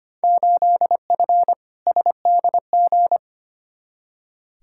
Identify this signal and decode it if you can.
Morse code, heard as a keyed tone, about 25 wpm, 700 Hz. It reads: 8F SDG
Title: Field Day Exchanges Encore - 25wpm